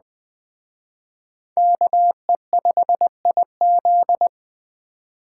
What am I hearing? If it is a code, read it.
Morse code, heard as a keyed tone, about 21 wpm, 700 Hz. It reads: KE5IZ
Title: Speed Racing - USA 2x2 Call Signs 17wpm